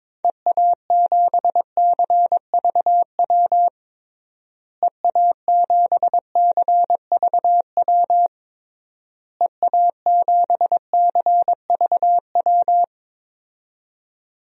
Morse code, heard as a keyed tone, about 22 wpm, 700 Hz. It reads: EA7CVW EA7CVW EA7CVW
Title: DXCC-100 Call Signs - 6 Characters - Repeated 3x - 22wpm